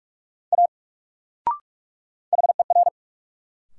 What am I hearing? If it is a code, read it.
Morse code, heard as a keyed tone, about 46 wpm, 700 Hz. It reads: A HER
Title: Sentences from Top 100 Words - New Word Drill - 45wpm